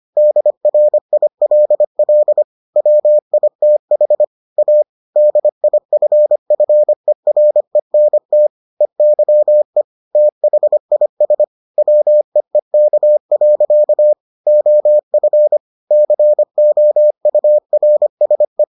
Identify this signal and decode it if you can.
Morse code, heard as a keyed tone, about 25 wpm, 600 Hz. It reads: DRILL WITH A DIFFERENT EYE THIS WEEK. OF COURSE